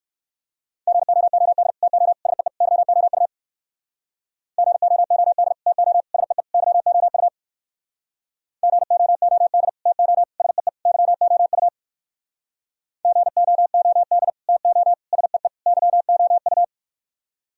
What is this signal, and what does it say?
Morse code, heard as a keyed tone, about 63 wpm, 700 Hz. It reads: GOOD TO SEE YOU GOOD TO SEE YOU GOOD TO SEE YOU GOOD TO SEE YOU